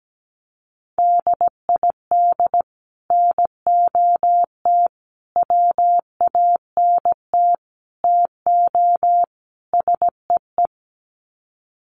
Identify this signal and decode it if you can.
Morse code, heard as a keyed tone, about 17 wpm, 700 Hz. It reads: DID NOT WANT TO SEE
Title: Sets of 5 Words Encore2 17wpm